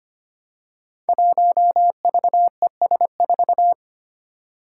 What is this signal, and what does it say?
Morse code, heard as a keyed tone, about 25 wpm, 700 Hz. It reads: 1VES4
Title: Five Challenging Letter-Number Combinations 25wpm